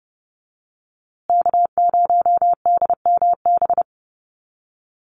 Morse code, heard as a keyed tone, about 30 wpm, 700 Hz. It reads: K0DMB
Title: USA Call Signs Repeated 2x 30wpm